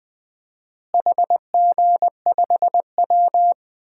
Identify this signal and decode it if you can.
Morse code, heard as a keyed tone, about 20 wpm, 700 Hz. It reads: HG5W
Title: DXCC-100 Call Signs - 4 Characters - Repeated 2x - 20wpm